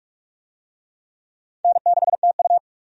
Morse code, heard as a keyed tone, about 46 wpm, 700 Hz. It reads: N6TU
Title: USA Call Signs - 4 Characters - 45wpm